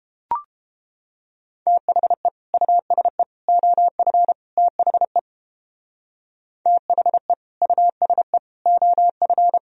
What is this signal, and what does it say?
Morse code, heard as a keyed tone, about 31 wpm, 700 Hz. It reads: THE USE OF THE THE USE OF